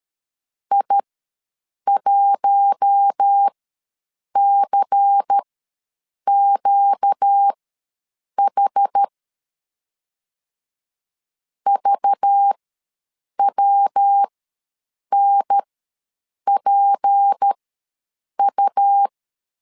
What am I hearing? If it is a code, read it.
Morse code, heard as a keyed tone, about 13 wpm, spaced as for 7 wpm, 800 Hz. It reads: I1CQH VWNPU
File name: LEZIONI / 6 PKX0 / 6 PK ws 12wpm 3116